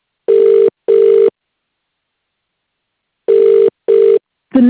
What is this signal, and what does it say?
Ringback tone (UK), 400 + 450 Hz.